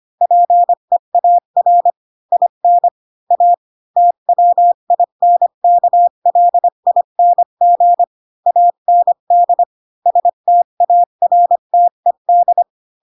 Morse code, heard as a keyed tone, about 25 wpm, 700 Hz. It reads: PEAR IN A TWINKLING AND STARTED
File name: Pinocchio / Pinocchio ch7 20 25 ik1hge